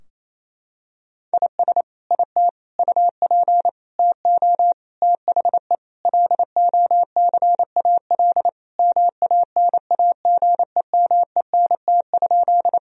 Morse code, heard as a keyed tone, about 28 wpm, 700 Hz. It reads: IS IT UP TO THE LOCAL MANAGEMENT?
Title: Sentences from Top 1000 Words - Review Part 4 - 28wpm